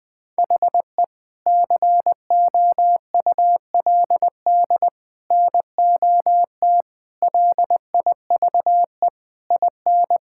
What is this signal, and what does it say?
Morse code, heard as a keyed tone, about 20 wpm, 700 Hz. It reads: HE COULD NOT LIVE IN
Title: Sentences from Top 1600 Words 20wpm